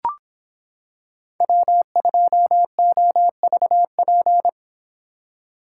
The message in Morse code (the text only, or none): W2OVP